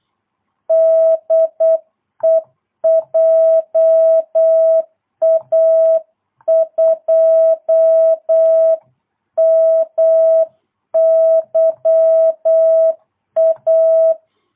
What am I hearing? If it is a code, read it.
Morse code, heard as a keyed tone, about 8 wpm, 650 Hz. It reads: DE JA2MYA